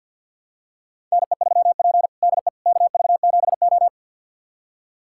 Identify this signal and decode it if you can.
Morse code, heard as a keyed tone, about 52 wpm, 700 Hz. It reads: NE3P DE KV7O
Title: USA Callsign DE Callsign - 4 Characters - 50wpm